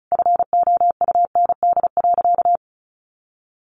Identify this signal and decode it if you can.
Morse code, heard as a keyed tone, about 34 wpm, 700 Hz. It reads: FOUND.
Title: Sentences from Top 1500 Words 35wpm